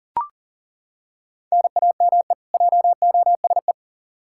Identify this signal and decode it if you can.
Morse code, heard as a keyed tone, about 37 wpm, 700 Hz. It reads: NAME JOSE
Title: Practice QSO - Exchange 1 - Beginning - 40wpm